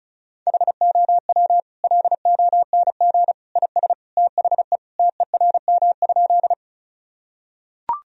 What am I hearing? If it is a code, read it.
Morse code, heard as a keyed tone, about 36 wpm, 700 Hz. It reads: HOW LONG IS THE TERM?
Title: Sentences from Top 1000 Words - Review Part 4 - 35wpm